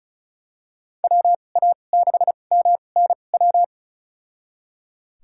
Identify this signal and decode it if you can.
Morse code, heard as a keyed tone, about 36 wpm, spaced as for 25 wpm, 700 Hz. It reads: WA6MNW
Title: USA Call Signs - 6 Characters - 2x Character Spacing - 35wpm